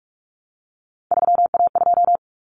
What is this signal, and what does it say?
Morse code, heard as a keyed tone, about 44 wpm, 700 Hz. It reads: 3A2
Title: Common Cut Numbers - 3-Digit 45wpm